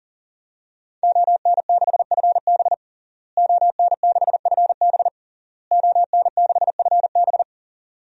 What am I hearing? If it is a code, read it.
Morse code, heard as a keyed tone, about 40 wpm, 700 Hz. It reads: ON6FB ON6FB ON6FB
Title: DXCC-100 Call Signs - 5 Characters - Repeated 3x - 40wpm